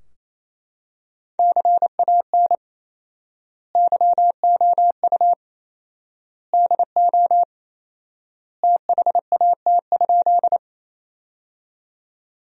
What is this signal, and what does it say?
Morse code, heard as a keyed tone, about 28 wpm, 700 Hz. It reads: CAN YOU DO THAT?
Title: 4x Word-Spacing - Sentences from Top 100 Words - 28wpm